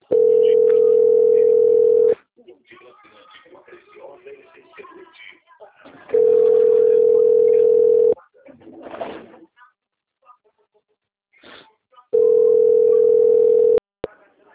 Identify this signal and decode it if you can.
Ringback tone (North America), 440 + 480 Hz, with speech.
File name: out- - - - - .6751